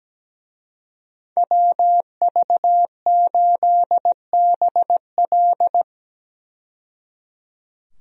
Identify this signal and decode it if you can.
Morse code, heard as a keyed tone, about 17 wpm, 700 Hz. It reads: WV8BL
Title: USA Call Signs - 5 Characters - 17wpm